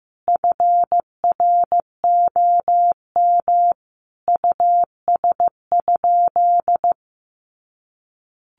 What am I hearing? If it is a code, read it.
Morse code, heard as a keyed tone, about 15 wpm, 700 Hz. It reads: FROM US?
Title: Sentences from Top 1500 Words 15wpm